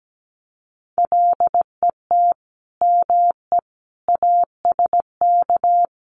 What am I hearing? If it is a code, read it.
Morse code, heard as a keyed tone, about 17 wpm, 700 Hz. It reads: LET ME ASK
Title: Sets of 5 Words 17wpm